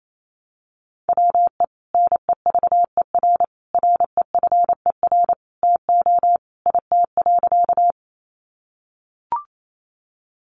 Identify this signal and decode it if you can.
Morse code, heard as a keyed tone, about 28 wpm, 700 Hz. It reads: WE NEVER REFER TO IT.
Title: Sentences from Top 1000 Words 28wpm